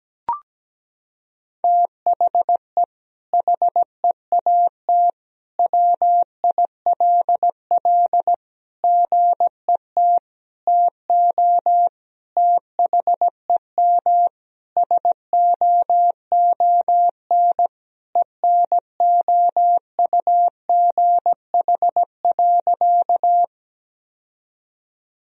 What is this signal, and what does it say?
Morse code, heard as a keyed tone, about 17 wpm, 700 Hz. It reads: THE HEAT WILL GET TO THEM SOON ENOUGH.
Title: Sentences from Top 1000 Words - Review Part 3 - 17wpm